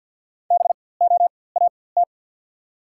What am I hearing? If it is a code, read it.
Morse code, heard as a keyed tone, about 54 wpm, spaced as for 24 wpm, 700 Hz. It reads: BOAT
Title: 4x Character Spacing - Top 500 Words - 50wpm